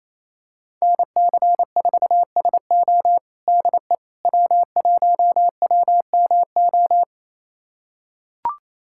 Morse code, heard as a keyed tone, about 28 wpm, 700 Hz. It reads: NC4SO DE W1WMO
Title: USA Callsign DE Callsign - 5 Characters - 28wpm